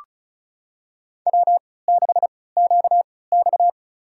Morse code, heard as a keyed tone, about 36 wpm, spaced as for 20 wpm, 700 Hz. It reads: W6QX
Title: USA Call Signs - 4 Characters - 3x Character Spacing - 35wpm